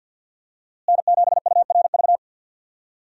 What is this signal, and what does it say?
Morse code, heard as a keyed tone, about 48 wpm, 700 Hz. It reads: N7URV
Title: USA Call Signs - 5 Characters - Repeated 3x - 50wpm